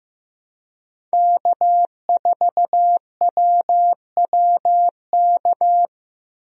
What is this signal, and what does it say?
Morse code, heard as a keyed tone, about 15 wpm, 700 Hz. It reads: K4WWK
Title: USA Call Signs - 5 Characters - 15wpm